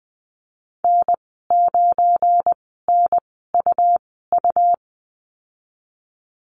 Morse code, heard as a keyed tone, about 20 wpm, spaced as for 15 wpm, 700 Hz. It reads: N9NUU
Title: 200 Lesson 32g - Letter k - Call Signs 20wpm